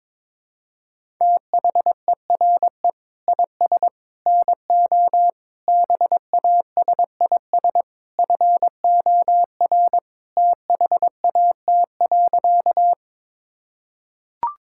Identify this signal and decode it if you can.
Morse code, heard as a keyed tone, about 22 wpm, 700 Hz. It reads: THERE IS NO BASIS FOR THAT.
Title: Sentences from Top 1000 Words - Review Part 5 - 22wpm